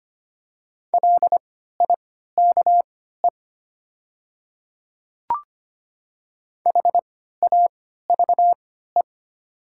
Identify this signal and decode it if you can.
Morse code, heard as a keyed tone, about 25 wpm, spaced as for 14 wpm, 700 Hz. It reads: LIKE HAVE
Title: 3x Character Spacing - Top 100 Words - 25wpm